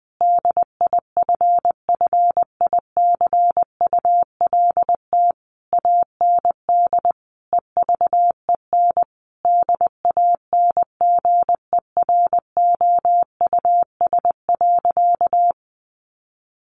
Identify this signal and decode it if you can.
Morse code, heard as a keyed tone, about 20 wpm, 700 Hz. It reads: DIFFICULT AND EVEN DANGEROUS.
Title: Sentences from Top 1200 Words 20wpm